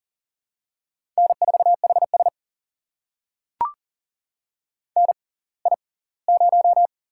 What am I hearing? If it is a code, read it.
Morse code, heard as a keyed tone, about 40 wpm, spaced as for 23 wpm, 700 Hz. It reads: N4HS NI0
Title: USA Call Signs - 4 Characters - 6x Character Spacing - 40wpm